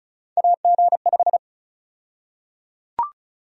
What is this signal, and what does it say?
Morse code, heard as a keyed tone, about 34 wpm, 700 Hz. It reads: AG5
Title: USA Call Sign Prefixes - Repeated 2x - 35wpm